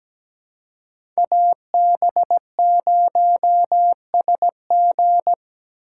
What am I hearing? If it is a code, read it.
Morse code, heard as a keyed tone, about 17 wpm, 700 Hz. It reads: AB0SG
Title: USA Call Signs - 2x Character Spacing - 17wpm